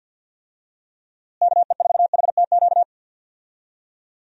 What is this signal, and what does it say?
Morse code, heard as a keyed tone, about 46 wpm, 700 Hz. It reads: KE4HTQ
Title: USA Call Signs - 6 Characters - Repeated 2x - 50wpm